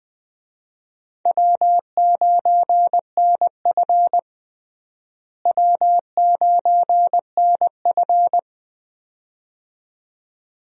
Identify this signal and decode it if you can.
Morse code, heard as a keyed tone, about 20 wpm, 700 Hz. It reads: W9NF W9NF
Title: USA Call Signs - 4 Characters - Repeated 2x - 20wpm